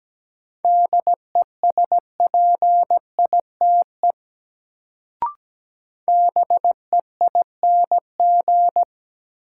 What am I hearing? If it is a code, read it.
Morse code, heard as a keyed tone, about 17 wpm, 700 Hz. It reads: DESPITE BEING
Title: Sets of 4 Words - New Word Drill - 17wpm